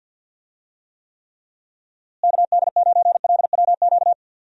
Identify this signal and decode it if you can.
Morse code, heard as a keyed tone, about 52 wpm, 700 Hz. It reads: KD9LWQ
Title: USA Call Signs - 6 Characters - Repeated 3x - 50wpm